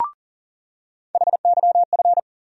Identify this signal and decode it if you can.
Morse code, heard as a keyed tone, about 40 wpm, 700 Hz. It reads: HYF
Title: USA Call Sign Postfixes - 3 Letters 40wpm